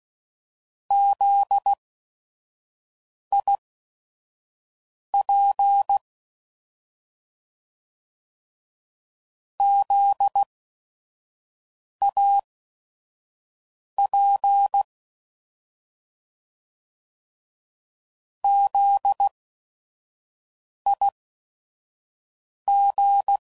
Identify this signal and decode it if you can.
Morse code, heard as a keyed tone, about 16 wpm, spaced as for 5 wpm, 800 Hz. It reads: ZIP ZAP ZIG
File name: Prefixes / By First Letter / Zs